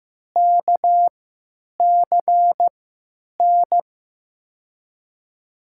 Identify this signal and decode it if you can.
Morse code, heard as a keyed tone, about 15 wpm, spaced as for 9 wpm, 700 Hz. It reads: KCN
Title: USA Call Signs - 6 Characters - 3x Character Spacing - 15wpm